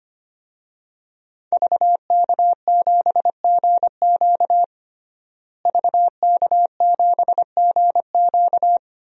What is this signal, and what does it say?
Morse code, heard as a keyed tone, about 25 wpm, 700 Hz. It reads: VK7GQ VK7GQ
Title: DXCC-100 Call Signs - 5 Characters - Repeated 2x - 25wpm